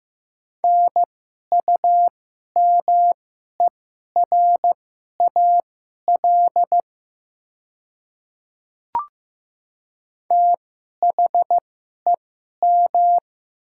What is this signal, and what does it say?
Morse code, heard as a keyed tone, about 15 wpm, spaced as for 11 wpm, 700 Hz. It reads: NUMERAL THEM
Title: 2x Character Spacing - Top 500 Words - 15wpm